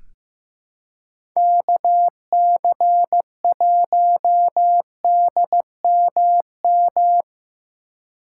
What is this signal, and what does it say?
Morse code, heard as a keyed tone, about 15 wpm, 700 Hz. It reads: KC1DMM
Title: USA Call Signs - 6 Characters - 4x Character Spacing - 15wpm